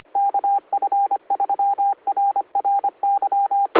Morse code, heard as a keyed tone, about 25 wpm, 800 Hz, with radio noise. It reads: KF3RRY